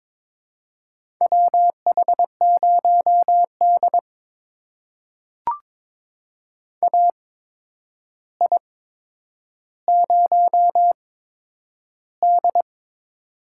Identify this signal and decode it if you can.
Morse code, heard as a keyed tone, about 21 wpm, spaced as for 9 wpm, 700 Hz. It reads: WH0D AI0D
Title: USA Call Signs - 4 Characters - 8x Character Spacing - 22wpm